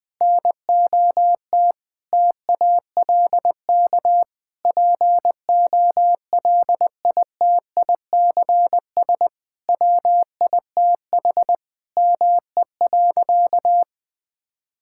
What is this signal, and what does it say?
Morse code, heard as a keyed tone, about 20 wpm, 700 Hz. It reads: NOT TALK POLITICS WITH ME.
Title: Sentences from Top 1000 Words - Review Part 6 - 20wpm